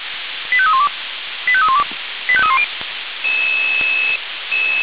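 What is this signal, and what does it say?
Five-tone selective call (ZVEI), heard as quick runs of five tones, tones of about 70 ms. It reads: 85312 85312 85310
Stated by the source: ZVEI five-tone selective call 85312 received by SDR.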